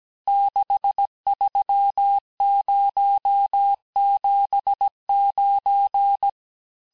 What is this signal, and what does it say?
Morse code, heard as a keyed tone, about 17 wpm, 800 Hz. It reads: 63079
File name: cours-du-lundi files / cours-1587-texte-chiffre-vitesse-16MPM